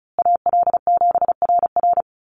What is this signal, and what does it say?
Morse code, heard as a keyed tone, about 34 wpm, 700 Hz. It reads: AL7RR
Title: USA Call Signs Repeated 3x 35wpm